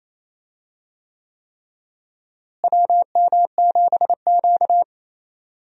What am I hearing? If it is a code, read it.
Morse code, heard as a keyed tone, about 28 wpm, 700 Hz. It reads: WM7Q
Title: USA Call Signs - 4 Characters - Repeated 3x - 28wpm